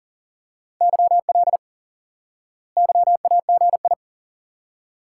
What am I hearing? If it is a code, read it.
Morse code, heard as a keyed tone, about 37 wpm, 700 Hz. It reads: YL YAGI
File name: IQE Course 3 - Lesson 89 - YAGI - Two Elements - 40wpm